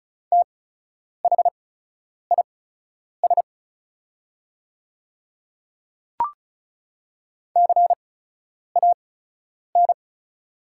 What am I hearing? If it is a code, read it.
Morse code, heard as a keyed tone, about 36 wpm, spaced as for 10 wpm, 700 Hz. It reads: THIS CAN
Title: 8x Character Spacing - Top 100 Words - 35wpm